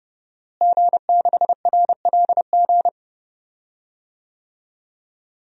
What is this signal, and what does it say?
Morse code, heard as a keyed tone, about 30 wpm, 700 Hz. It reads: G6RLG
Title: DXCC-100 Call Signs - 5 Characters - Repeated 3x - 30wpm